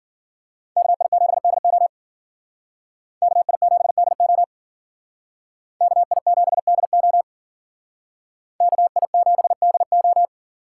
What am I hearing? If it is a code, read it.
Morse code, heard as a keyed tone, about 51 wpm, 700 Hz. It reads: KI7DO KI7DO KI7DO KI7DO